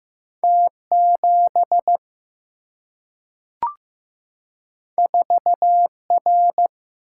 Morse code, heard as a keyed tone, about 15 wpm, 700 Hz. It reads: T 7 4 R
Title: Single-Letter Single-Number 15wpm